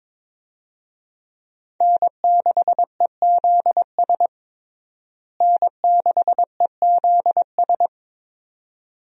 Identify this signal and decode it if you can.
Morse code, heard as a keyed tone, about 21 wpm, 700 Hz. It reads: N6EZS N6EZS
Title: USA Call Signs - 5 Characters - Repeated 2x - 22wpm